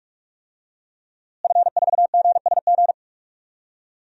Morse code, heard as a keyed tone, about 46 wpm, 700 Hz. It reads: F4GSG
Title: DXCC-100 Call Signs - 5 Characters - Repeated 3x - 45wpm